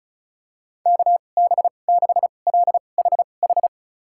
Morse code, heard as a keyed tone, about 36 wpm, spaced as for 25 wpm, 700 Hz. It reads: KB6LHH
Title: USA Call Signs - 6 Characters - 2x Character Spacing - 35wpm